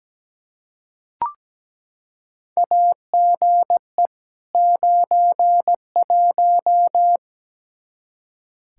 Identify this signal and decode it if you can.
Morse code, heard as a keyed tone, about 17 wpm, 700 Hz. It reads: AGE 91
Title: Practice QSO - Exchange 3 - More Info - 17wpm